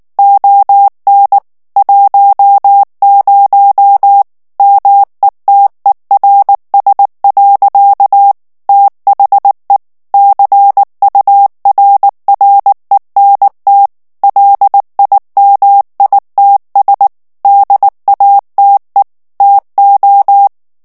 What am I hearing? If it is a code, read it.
Morse code, heard as a keyed tone, about 19 wpm, 800 Hz. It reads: ON 10 METERS. THE CURRENT LIMITS DATE TO
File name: ANGLAIS texte clair 3 vitesse 18 mots minute